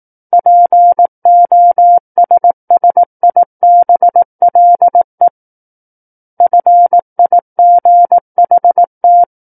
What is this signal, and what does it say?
Morse code, heard as a keyed tone, about 18 wpm, 700 Hz. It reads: POSSIBLE FIGHT
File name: Book One / 09-Chapter 9 - 18 WPM